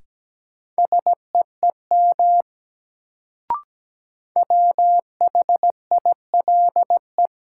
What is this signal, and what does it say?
Morse code, heard as a keyed tone, about 17 wpm, 700 Hz. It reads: SEEM WHILE
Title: Sets of 3 Words - New Word Drill - 17wpm